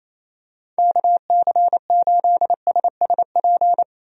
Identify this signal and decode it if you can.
Morse code, heard as a keyed tone, about 28 wpm, 700 Hz. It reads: KC8SSP
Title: USA Call Signs - 6 Characters - 28wpm